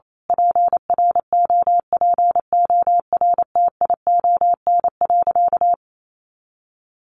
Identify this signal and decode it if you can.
Morse code, heard as a keyed tone, about 28 wpm, 700 Hz. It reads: PROPORTION.
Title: A Sound Of Thunder 28wpm